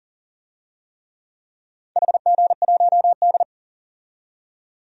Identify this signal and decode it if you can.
Morse code, heard as a keyed tone, about 40 wpm, 700 Hz. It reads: HG1D
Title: DXCC-100 Call Signs - 4 Characters - 40wpm